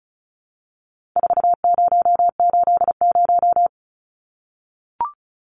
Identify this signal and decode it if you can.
Morse code, heard as a keyed tone, about 34 wpm, 700 Hz. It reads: 4080
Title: Numbers - 4-Digit 35wpm